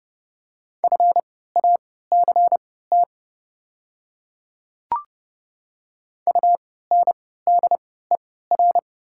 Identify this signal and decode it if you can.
Morse code, heard as a keyed tone, about 30 wpm, spaced as for 17 wpm, 700 Hz. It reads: FACT UNDER
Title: 3x Character Spacing - Top 500 Words - 30wpm